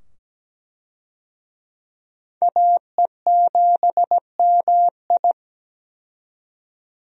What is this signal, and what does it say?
Morse code, heard as a keyed tone, about 17 wpm, 700 Hz. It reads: AE7MI
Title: USA Call Signs - 5 Characters - 17wpm